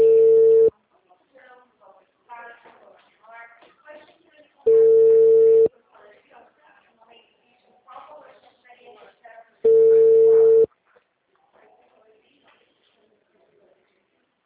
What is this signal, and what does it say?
Ringback tone (China, 450 Hz), with speech.
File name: out- -2005- - - .58437